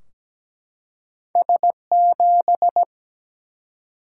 Morse code, heard as a keyed tone, about 17 wpm, 700 Hz. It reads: S7
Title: Single-Letter Single-Number 17wpm